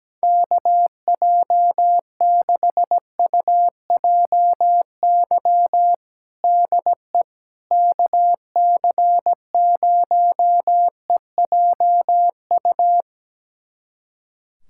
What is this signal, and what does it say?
Morse code, heard as a keyed tone, about 17 wpm, 700 Hz. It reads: KJ6UJY DE KC0EJU
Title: USA Callsign DE Callsign - 6 Characters - 17wpm